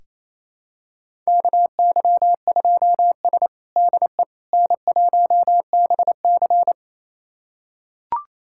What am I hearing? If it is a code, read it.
Morse code, heard as a keyed tone, about 28 wpm, 700 Hz. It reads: KY2S DE N1BC
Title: USA Callsign DE Callsign - 4 Characters - 28wpm